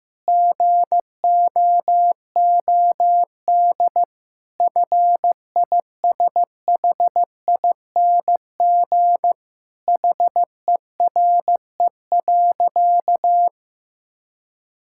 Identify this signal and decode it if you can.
Morse code, heard as a keyed tone, about 15 wpm, 700 Hz. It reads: GOOD FISHING HERE.
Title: Sentences from Top 1000 Words - Review Part 5 - 15wpm